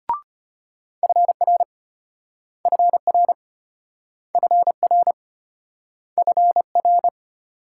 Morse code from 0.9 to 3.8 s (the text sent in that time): FR FR